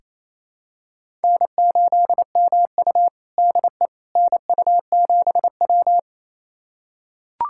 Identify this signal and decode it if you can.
Morse code, heard as a keyed tone, about 28 wpm, 700 Hz. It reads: N8MU DE NU7W
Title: USA Callsign DE Callsign - 4 Characters - 28wpm